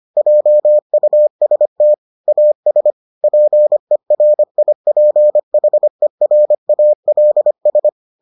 Morse code, heard as a keyed tone, about 25 wpm, 600 Hz. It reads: JUST AS PERIPHERALS